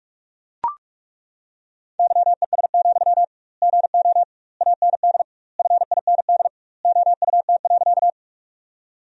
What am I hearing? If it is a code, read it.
Morse code, heard as a keyed tone, about 42 wpm, 700 Hz. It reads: YES, GO AND FIND OUT.
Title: 2x Word-Spacing - Sentences from Top 500 Words - 45wpm